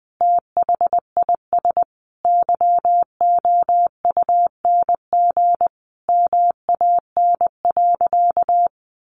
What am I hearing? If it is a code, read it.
Morse code, heard as a keyed tone, about 20 wpm, 700 Hz. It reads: THIS YOUNG MAN.
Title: Sentences from Top 1300 Words 20wpm